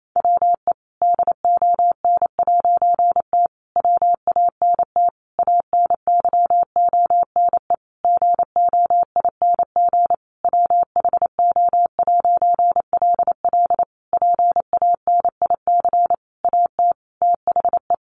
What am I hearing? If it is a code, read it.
Morse code, heard as a keyed tone, about 28 wpm, 700 Hz. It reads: WE DON'T WANT ANYONE GOING WHO'LL PANIC AT THE
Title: A Sound Of Thunder 28wpm